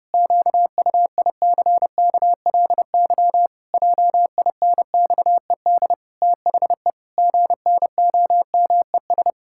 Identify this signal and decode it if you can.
Morse code, heard as a keyed tone, about 30 wpm, 700 Hz. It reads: QUICKLY JINXED THE GNOMES
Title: Pangram Challenge v2 - 30wpm